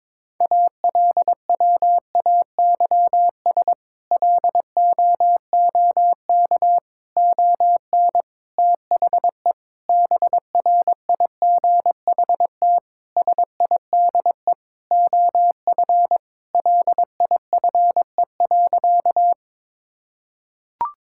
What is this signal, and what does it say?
Morse code, heard as a keyed tone, about 22 wpm, 700 Hz. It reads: ALWAYS LOOK ON THE BRIGHT SIDE OF LIFE.
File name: practice / 22wpm / Sentences with 8-Word Phrases - 22wpm